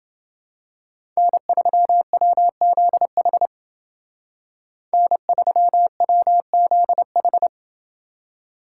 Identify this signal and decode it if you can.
Morse code, heard as a keyed tone, about 29 wpm, 700 Hz. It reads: N3WZH N3WZH